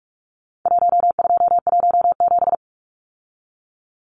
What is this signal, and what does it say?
Morse code, heard as a keyed tone, about 44 wpm, 700 Hz. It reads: 1217
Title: Numbers - Contest Style - 4-Digit 45wpm